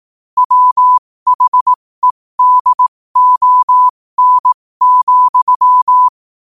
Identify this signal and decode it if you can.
Morse code, heard as a keyed tone, about 18 wpm, spaced as for 15 wpm, 1000 Hz. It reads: WHEDON,
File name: qotdfiles / 15wpm